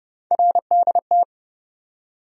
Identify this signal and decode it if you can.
Morse code, heard as a keyed tone, about 31 wpm, 700 Hz. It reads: RDT
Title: USA Call Sign Postfixes - 3 Letters 30wpm